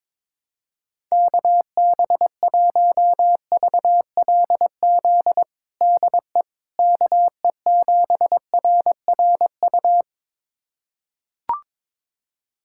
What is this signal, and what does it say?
Morse code, heard as a keyed tone, about 22 wpm, 700 Hz. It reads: KB1VLZ DE KE7RRU
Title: USA Callsign DE Callsign 22wpm